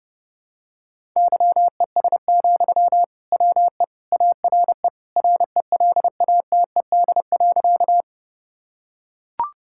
Morse code, heard as a keyed tone, about 32 wpm, 700 Hz. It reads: YES, WE ARE RELATED.
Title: Sentences from Top 1000 Words - Review Part 5 - 30wpm